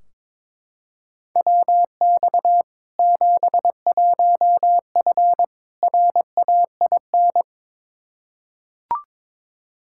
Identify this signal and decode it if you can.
Morse code, heard as a keyed tone, about 22 wpm, 700 Hz. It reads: WX 71F RAIN
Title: K1USN SST QSO Practice 22wpm